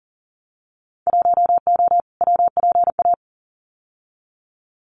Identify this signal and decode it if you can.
Morse code, heard as a keyed tone, about 40 wpm, 700 Hz. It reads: 1O WPA
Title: Winter Field Day Exchanges 40wpm